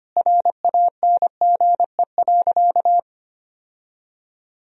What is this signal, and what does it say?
Morse code, heard as a keyed tone, about 25 wpm, 700 Hz. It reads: RANGE.
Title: Sentences from Top 1000 Words - Review Part 4 - 25wpm